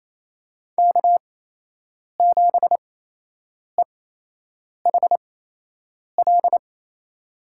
Morse code, heard as a keyed tone, about 28 wpm, spaced as for 8 wpm, 700 Hz. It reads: K7EHL
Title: USA Call Signs - 5 Characters - 8x Character Spacing - 28wpm